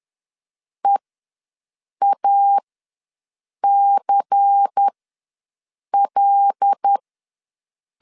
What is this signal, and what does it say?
Morse code, heard as a keyed tone, about 10 wpm, spaced as for 6 wpm, 800 Hz. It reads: EACL
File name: LEZIONI / 8 YJ45 / 8 YJ45 ws 10wpm 3843